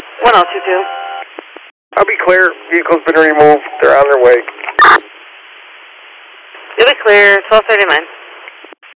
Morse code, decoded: none

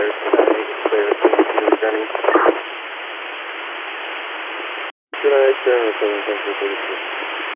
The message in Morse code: none

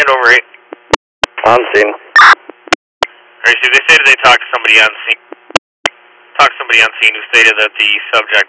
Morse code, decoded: none

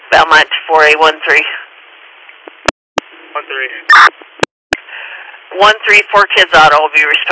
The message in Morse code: none

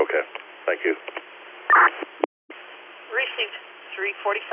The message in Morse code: none

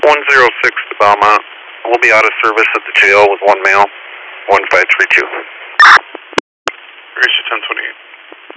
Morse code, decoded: none

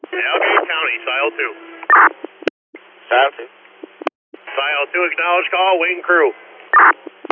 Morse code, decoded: none

